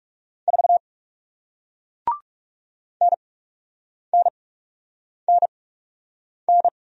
Morse code, heard as a keyed tone, about 41 wpm, spaced as for 8 wpm, 700 Hz. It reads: 4 NNNN